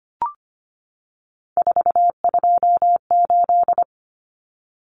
Morse code, heard as a keyed tone, about 25 wpm, 700 Hz. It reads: 428
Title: Common Cut Numbers - 3-Digit 25wpm